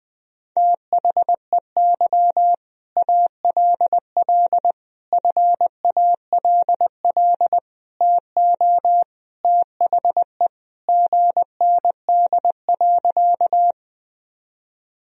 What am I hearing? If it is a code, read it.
Morse code, heard as a keyed tone, about 20 wpm, 700 Hz. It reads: THEY ALL FALL TO THE GND.
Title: Sentences from Top 500 Words and Abbreviations 20wpm